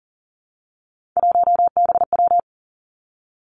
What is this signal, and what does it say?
Morse code, heard as a keyed tone, about 39 wpm, 700 Hz. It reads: 1BW
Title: Three Challenging Letter-Number Combinations 40wpm